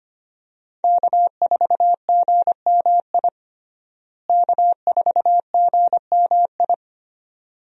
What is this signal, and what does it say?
Morse code, heard as a keyed tone, about 25 wpm, 700 Hz. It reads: K4GMI K4GMI
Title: USA Call Signs - 5 Characters - Repeated 2x - 25wpm